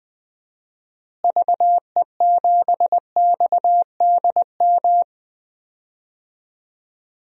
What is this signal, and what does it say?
Morse code, heard as a keyed tone, about 20 wpm, 700 Hz. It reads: VE7XDM
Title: DXCC-100 Call Signs - 6 Characters - Repeated 3x - 20wpm